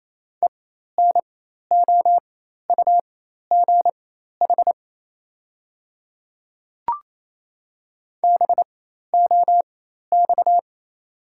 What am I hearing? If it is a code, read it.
Morse code, heard as a keyed tone, about 28 wpm, spaced as for 13 wpm, 690 Hz. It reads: ENOUGH BOX
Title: 4x Character Spacing - Top 500 Words - 28wpm